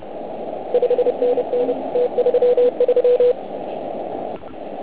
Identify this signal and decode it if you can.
Morse code, heard as a keyed tone, about 29 wpm, 500 Hz, with noise. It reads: 5NN T33
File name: Archiv / CUC129 / CUC129 pileup